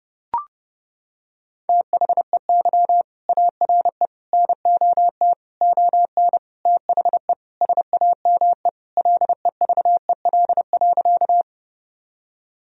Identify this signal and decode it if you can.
Morse code, heard as a keyed tone, about 30 wpm, 700 Hz. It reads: THEY ARE NOT ON THE SAME LEVEL.
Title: Sentences from Top 1000 Words - Review Part 4 - 30wpm